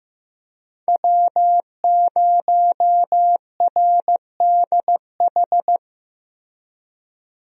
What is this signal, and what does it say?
Morse code, heard as a keyed tone, about 15 wpm, 700 Hz. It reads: W0RDH
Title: USA Call Signs - 5 Characters - Repeated 2x - 15wpm